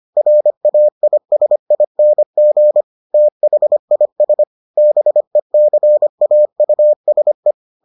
Morse code, heard as a keyed tone, about 25 wpm, 600 Hz. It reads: RAISING THIS BECAUSE